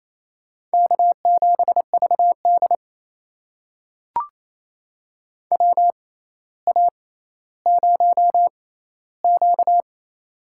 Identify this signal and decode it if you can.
Morse code, heard as a keyed tone, about 28 wpm, spaced as for 14 wpm, 700 Hz. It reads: K7VD WA0Q
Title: USA Call Signs - 4 Characters - 6x Character Spacing - 28wpm